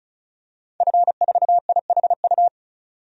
Morse code, heard as a keyed tone, about 36 wpm, 700 Hz. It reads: F4IHU
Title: DXCC-100 Call Signs - 5 Characters - 35wpm